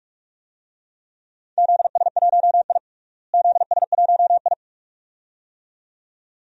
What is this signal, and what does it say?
Morse code, heard as a keyed tone, about 46 wpm, 700 Hz. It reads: ZS1I ZS1I
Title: DXCC-100 Call Signs - 4 Characters - Repeated 2x - 45wpm